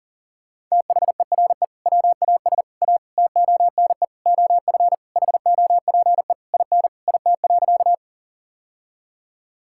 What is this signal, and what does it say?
Morse code, heard as a keyed tone, about 40 wpm, 700 Hz. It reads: THERE WAS A TONE OF HOPE IN IT.
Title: Sentences from Top 1600 Words 40wpm